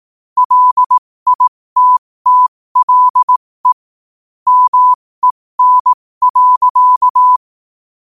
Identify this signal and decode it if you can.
Morse code, heard as a keyed tone, about 18 wpm, spaced as for 15 wpm, 1000 Hz. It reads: LITTLE MEN.
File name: qotdfiles / 15wpm